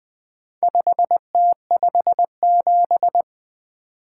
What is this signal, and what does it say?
Morse code, heard as a keyed tone, about 20 wpm, 700 Hz. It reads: 5T57
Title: Four Letter-Number Combinations - 20wpm